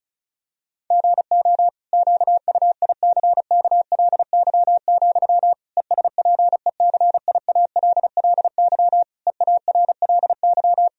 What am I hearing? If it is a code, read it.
Morse code, heard as a keyed tone, about 34 wpm, 700 Hz. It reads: GO QUICKLY, ESPECIALLY EARLY
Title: Sentences from Top 1000 Words - Review Part 4 - 35wpm